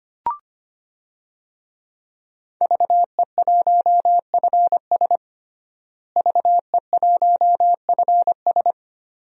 Morse code, heard as a keyed tone, about 25 wpm, 700 Hz. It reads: VE1FS VE1FS
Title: DXCC-100 Call Signs - 5 Characters - Repeated 2x - 25wpm